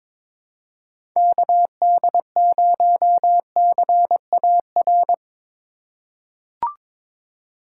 Morse code, heard as a keyed tone, about 22 wpm, 700 Hz. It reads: KD0CAR
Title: USA Call Signs - 4x Character Spacing - 22wpm